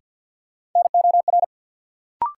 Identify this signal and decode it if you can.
Morse code, heard as a keyed tone, about 52 wpm, 700 Hz. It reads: NOR
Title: Sentences from Top 1500 Words - New Word Drill Alt - 50wpm